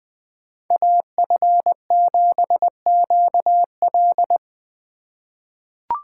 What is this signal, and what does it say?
Morse code, heard as a keyed tone, about 20 wpm, 700 Hz. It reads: AF7QL
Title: USA Call Signs - 5 Characters - 4x Character Spacing - 20wpm